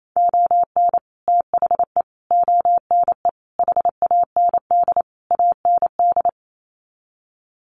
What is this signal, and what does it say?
Morse code, heard as a keyed tone, about 28 wpm, 700 Hz. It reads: ON THE ONE HAND AND
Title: Sets of 5 Words Encore2 28wpm